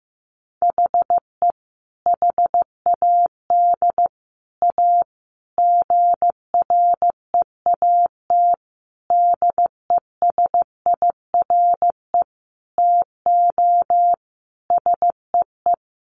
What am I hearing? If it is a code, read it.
Morse code, heard as a keyed tone, about 15 wpm, 700 Hz. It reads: HE HAD A GREAT DESIRE TO SEE
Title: Sentences from Top 1100 Words 15wpm